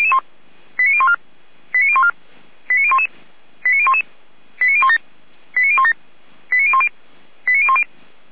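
Five-tone selective call (ZVEI), heard as quick runs of five tones, tones of about 70 ms. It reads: 89014 89014 89011 89011 89017 89017 89010 89010
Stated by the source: ZVEI five-tone selective call 89060, received by SDR.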